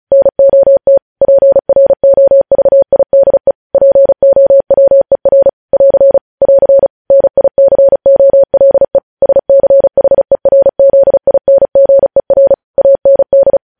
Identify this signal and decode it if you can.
Morse code, heard as a keyed tone, about 36 wpm, 550 Hz. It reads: NOT PROVIDE POWER <AR> <AR> NICOLE SCHERZINGER AND
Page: news headlines in morse code at 35 wpm